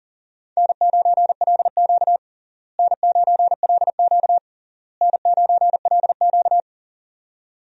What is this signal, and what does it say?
Morse code, heard as a keyed tone, about 40 wpm, 700 Hz. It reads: N9LQ N9LQ N9LQ
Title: USA Call Signs - 4 Characters - Repeated 3x - 40wpm